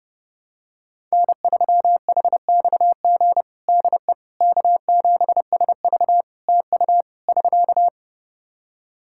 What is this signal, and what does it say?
Morse code, heard as a keyed tone, about 30 wpm, 700 Hz. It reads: N3HXG DE K7SV TU <SK>
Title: K1USN SST QSO Practice 30wpm